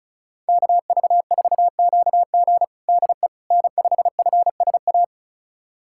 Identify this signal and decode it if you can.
Morse code, heard as a keyed tone, about 36 wpm, 700 Hz. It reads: KV4QG DE N5FSA
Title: USA Callsign DE Callsign - 5 Characters - 35wpm